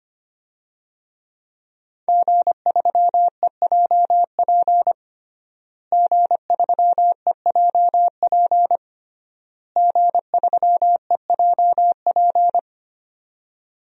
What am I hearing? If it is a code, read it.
Morse code, heard as a keyed tone, about 25 wpm, 700 Hz. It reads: G3EJP G3EJP G3EJP
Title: DXCC-100 Call Signs - 5 Characters - Repeated 3x - 25wpm